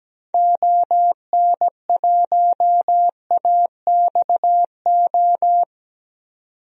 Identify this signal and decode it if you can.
Morse code, heard as a keyed tone, about 17 wpm, 700 Hz. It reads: ON1AXO
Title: DXCC-100 Call Signs - 6 Characters - 17wpm